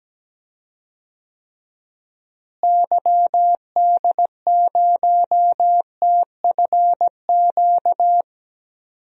Morse code, heard as a keyed tone, about 17 wpm, 700 Hz. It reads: YD0TFQ
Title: DXCC-100 Call Signs - 6 Characters - Repeated 3x - 17wpm